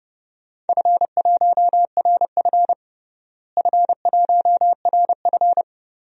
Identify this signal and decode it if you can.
Morse code, heard as a keyed tone, about 31 wpm, 700 Hz. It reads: F1RF F1RF
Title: DXCC-100 Call Signs - 4 Characters - Repeated 3x - 30wpm